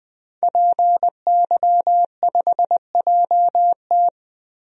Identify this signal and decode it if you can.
Morse code, heard as a keyed tone, about 20 wpm, 700 Hz. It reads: PY5JT
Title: DXCC-100 Call Signs - 5 Characters - 20wpm